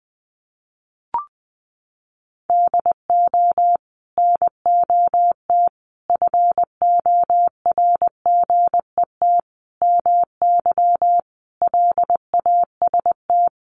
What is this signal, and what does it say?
Morse code, heard as a keyed tone, about 20 wpm, 700 Hz. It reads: DO NOT FORGET MY LAST
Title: Sentences from Top 1200 Words 20wpm